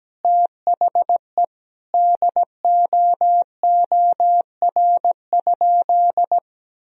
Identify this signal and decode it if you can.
Morse code, heard as a keyed tone, about 17 wpm, 700 Hz. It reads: THE DOOR?
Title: Sentences from Top 1000 Words - Review Part 4 - 17wpm